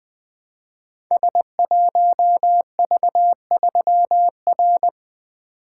Morse code, heard as a keyed tone, about 20 wpm, 700 Hz. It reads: S1V3R
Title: Code Groups - 5 Letter:Numbers - 20wpm